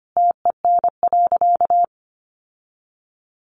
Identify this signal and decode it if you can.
Morse code, heard as a keyed tone, about 25 wpm, 700 Hz. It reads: TEN.
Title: Sentences from Top 600 Words 25wpm